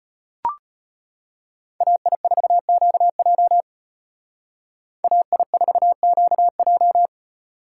Morse code, heard as a keyed tone, about 32 wpm, 700 Hz. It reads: AI4QJ AI4QJ